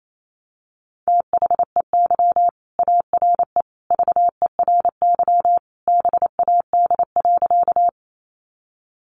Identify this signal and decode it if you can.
Morse code, heard as a keyed tone, about 28 wpm, 700 Hz. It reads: THEY ARE VERY BAD.
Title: Sentences from Top 700 Words 28wpm